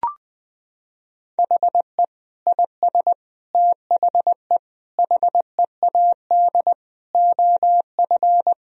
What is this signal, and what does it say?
Morse code, heard as a keyed tone, about 21 wpm, 700 Hz. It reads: HE IS THE HEAD OF
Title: Sentences from Top 1000 Words - Review Part 4 - 20wpm